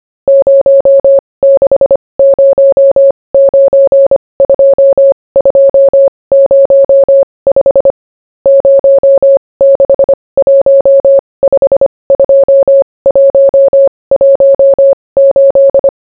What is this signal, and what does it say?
Morse code, heard as a keyed tone, about 25 wpm, spaced as for 20 wpm, 550 Hz. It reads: 06092205 06152118